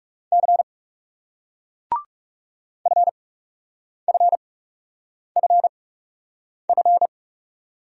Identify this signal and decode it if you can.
Morse code, heard as a keyed tone, about 34 wpm, spaced as for 8 wpm, 700 Hz. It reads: C FFFF
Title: Speed-Racing Single Letters 30wpm